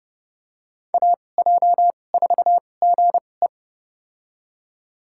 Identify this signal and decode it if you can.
Morse code, heard as a keyed tone, about 30 wpm, spaced as for 22 wpm, 700 Hz. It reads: AJ4GE
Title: USA Call Signs - 2x Character Spacing - 30wpm